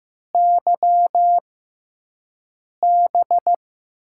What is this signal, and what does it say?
Morse code, heard as a keyed tone, about 15 wpm, spaced as for 5 wpm, 700 Hz. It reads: YB
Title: USA Call Signs - 6x Character Spacing - 15wpm